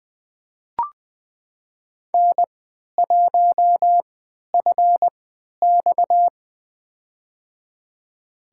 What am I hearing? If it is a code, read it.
Morse code, heard as a keyed tone, about 20 wpm, spaced as for 11 wpm, 700 Hz. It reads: N1FX
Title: USA Call Signs - 4 Characters - 3x Character Spacing - 20wpm